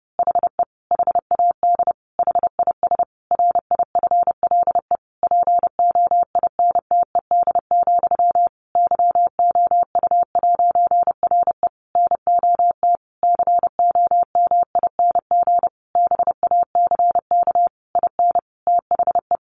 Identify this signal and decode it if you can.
Morse code, heard as a keyed tone, about 30 wpm, 700 Hz. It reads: HE HAD HIS RIFLE POINTED, YOU'RE NOT COMING BACK IN THE
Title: A Sound Of Thunder 30wpm